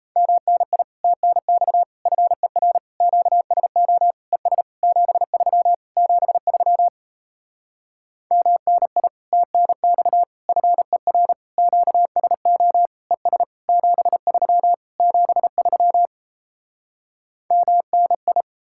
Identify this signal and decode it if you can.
Morse code, heard as a keyed tone, about 37 wpm, 700 Hz. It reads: MNI TNX FER QSO ES 73 73 MNI TNX FER QSO ES 73 73 MNI